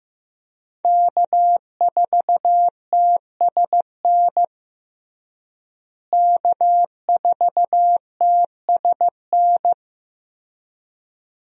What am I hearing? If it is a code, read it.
Morse code, heard as a keyed tone, about 15 wpm, 700 Hz. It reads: K4TSN K4TSN
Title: USA Call Signs - 5 Characters - Repeated 2x - 15wpm